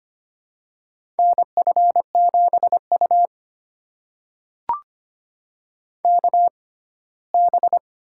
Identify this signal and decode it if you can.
Morse code, heard as a keyed tone, about 25 wpm, spaced as for 17 wpm, 700 Hz. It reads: NF7U KB
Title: USA Call Signs - 6x Character Spacing - 25wpm